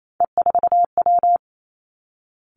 Morse code, heard as a keyed tone, about 28 wpm, 700 Hz. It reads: E4W
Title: Three Challenging Letter-Number Combinations 28wpm